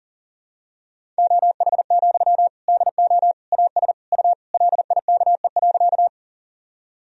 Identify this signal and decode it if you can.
Morse code, heard as a keyed tone, about 40 wpm, 700 Hz. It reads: OH, DO AS U LIKE.
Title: Sentences from Top 500 Words and Abbreviations 40wpm